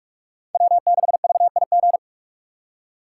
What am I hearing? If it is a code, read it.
Morse code, heard as a keyed tone, about 47 wpm, 700 Hz. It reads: W6VIG
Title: USA Call Signs - 5 Characters - 4x Character Spacing - 45wpm